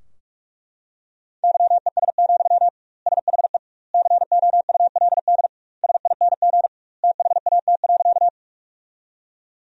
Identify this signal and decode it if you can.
Morse code, heard as a keyed tone, about 41 wpm, 700 Hz. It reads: YES, SHE COULD SING THAT.
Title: 2x Word-Spacing - Sentences from Top 500 Words - 45wpm